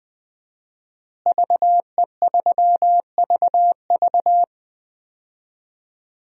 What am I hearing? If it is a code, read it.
Morse code, heard as a keyed tone, about 20 wpm, 700 Hz. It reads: VE3VV
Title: DXCC-100 Call Signs - 5 Characters - Repeated 3x - 20wpm